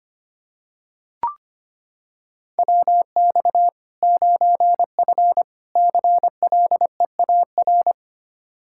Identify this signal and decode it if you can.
Morse code, heard as a keyed tone, about 25 wpm, 700 Hz. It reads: WX 9F CLEAR
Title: K1USN SST QSO Practice 25wpm